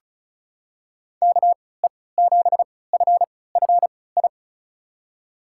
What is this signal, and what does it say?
Morse code, heard as a keyed tone, about 36 wpm, spaced as for 20 wpm, 700 Hz. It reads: KE7FFI
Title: USA Call Signs - 3x Character Spacing - 35wpm